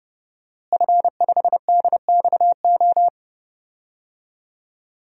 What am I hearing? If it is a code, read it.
Morse code, heard as a keyed tone, about 30 wpm, 700 Hz. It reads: F5DXO
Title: DXCC-100 Call Signs - 5 Characters - Repeated 3x - 30wpm